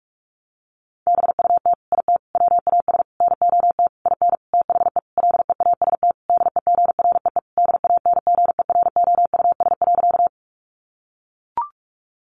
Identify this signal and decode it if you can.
Morse code, heard as a keyed tone, about 44 wpm, 700 Hz. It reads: BUT IT WAS NOT IN THE LEAST DEGREE DANGEROUS.
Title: Sentences from Top 1500 Words 45wpm